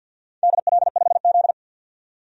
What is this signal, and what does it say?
Morse code, heard as a keyed tone, about 51 wpm, 700 Hz. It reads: DL5Z
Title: DXCC-100 Call Signs - 4 Characters - 50wpm